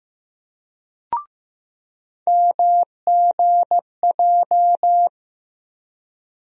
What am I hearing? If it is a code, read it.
Morse code, heard as a keyed tone, about 15 wpm, 700 Hz. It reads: MGJ
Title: USA Call Sign Postfixes - 3 Letters 15wpm